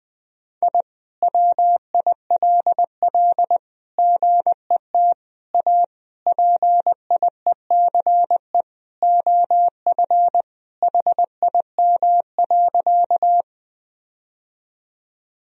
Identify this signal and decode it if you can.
Morse code, heard as a keyed tone, about 20 wpm, 700 Hz. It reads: I WILL GET A PIECE OF HIM.
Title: Sentences from Top 1000 Words - Review Part 2 - 20wpm